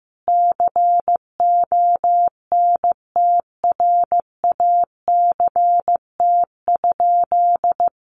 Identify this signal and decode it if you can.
Morse code, heard as a keyed tone, about 14 wpm, 700 Hz. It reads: CONTRACT?
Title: Sentences from Top 1400 Words 15wpm